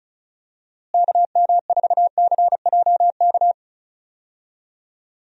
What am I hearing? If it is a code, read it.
Morse code, heard as a keyed tone, about 34 wpm, 700 Hz. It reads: KM4CJK
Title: USA Call Signs - 6 Characters - Repeated 3x - 35wpm